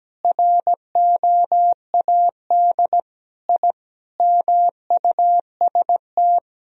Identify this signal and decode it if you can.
Morse code, heard as a keyed tone, about 17 wpm, 700 Hz. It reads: ROAD I MUST
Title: Sentences from Top 1000 Words - Review Part 2 - 17wpm